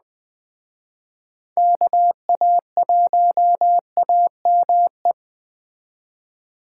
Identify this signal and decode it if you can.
Morse code, heard as a keyed tone, about 20 wpm, 700 Hz. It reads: KA1AME
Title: Speed Racing - USA 2x3 Call Signs 17wpm